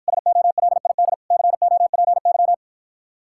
Morse code, heard as a keyed tone, about 54 wpm, 700 Hz. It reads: SOLID COPY